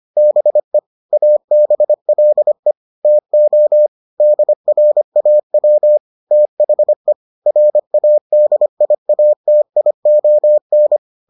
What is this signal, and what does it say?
Morse code, heard as a keyed tone, about 25 wpm, 600 Hz. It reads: BE ABLE TO DRAW THE RADIATION